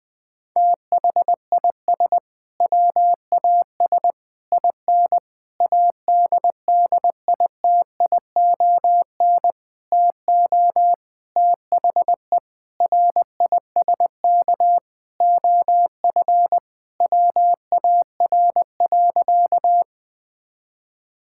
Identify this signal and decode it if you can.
Morse code, heard as a keyed tone, about 21 wpm, 700 Hz. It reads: THIS WAS IN ADDITION TO THE RISK OF WAR.
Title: Sentences from Top 1000 Words - Review Part 6 - 20wpm